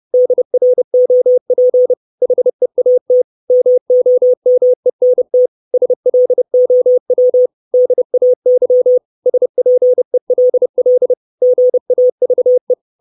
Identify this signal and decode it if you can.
Morse code, heard as a keyed tone, about 30 wpm, 500 Hz. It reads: DROP HEAT MOMENT SLOW DAY SPELL GAVE
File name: 632 Words 30WPM